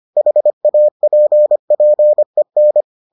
Morse code, heard as a keyed tone, about 25 wpm, 600 Hz. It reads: HAPPEN